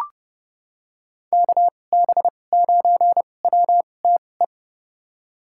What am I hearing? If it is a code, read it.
Morse code, heard as a keyed tone, about 30 wpm, spaced as for 22 wpm, 700 Hz. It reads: KB9WTE
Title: USA Call Signs - 6 Characters - 2x Character Spacing - 30wpm